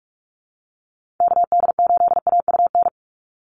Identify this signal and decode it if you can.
Morse code, heard as a keyed tone, about 44 wpm, 700 Hz. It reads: KD8AUN
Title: USA Call Signs 45wpm